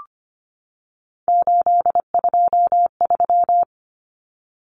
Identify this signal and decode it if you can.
Morse code, heard as a keyed tone, about 25 wpm, 700 Hz. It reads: 823
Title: Common Cut Numbers - 3-Digit 25wpm